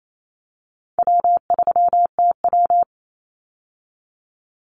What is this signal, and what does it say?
Morse code, heard as a keyed tone, about 28 wpm, 700 Hz. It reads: W3TW
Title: Easy Call Signs Repeated 3x - 28wpm